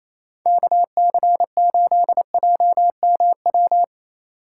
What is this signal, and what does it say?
Morse code, heard as a keyed tone, about 33 wpm, 700 Hz. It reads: KC8JMW
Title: Speed Racing - USA Call Signs 28wpm